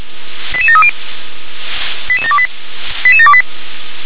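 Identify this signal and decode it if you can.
Five-tone selective call (ZVEI), heard as quick runs of five tones, tones of about 70 ms. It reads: 88730 88728 88728
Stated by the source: ZVEI five-tone selective call received with an SDR.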